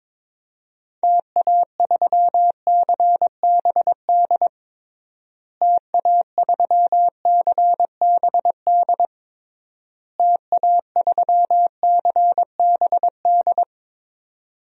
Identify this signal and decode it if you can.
Morse code, heard as a keyed tone, about 22 wpm, 700 Hz. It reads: TA3CBD TA3CBD TA3CBD
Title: DXCC-100 Call Signs - 6 Characters - Repeated 3x - 22wpm